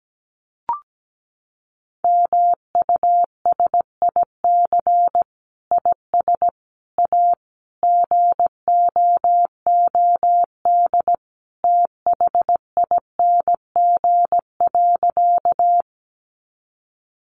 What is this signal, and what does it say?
Morse code, heard as a keyed tone, about 17 wpm, 700 Hz. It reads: MUSIC IS A GOOD THING.
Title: Sentences from Top 300 Words 17wpm